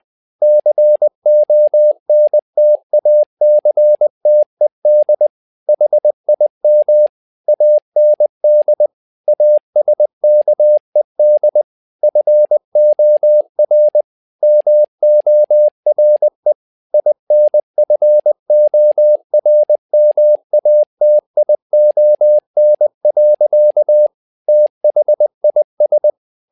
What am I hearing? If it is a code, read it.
Morse code, heard as a keyed tone, about 20 wpm, 600 Hz. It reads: CONTACTED HIM AND ASKED FOR MORE INFORMATION. THIS
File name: learning center / tower trouble / tower trouble 20wpm